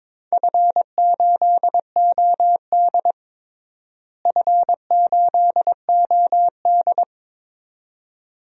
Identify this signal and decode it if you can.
Morse code, heard as a keyed tone, about 22 wpm, 700 Hz. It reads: F8OD F8OD
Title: DXCC-100 Call Signs - 4 Characters - Repeated 2x - 22wpm